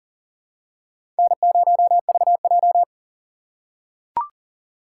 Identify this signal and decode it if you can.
Morse code, heard as a keyed tone, about 40 wpm, 700 Hz. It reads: N0VJ
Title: USA Call Signs - 4 Characters - 3x Character Spacing - 40wpm